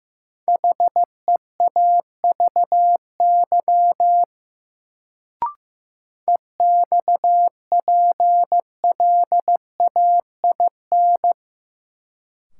Morse code, heard as a keyed tone, about 15 wpm, 700 Hz. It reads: HEAVY EXPLAIN
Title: Sentences from Top 1100 Words - New Word Drill Alt - 15wpm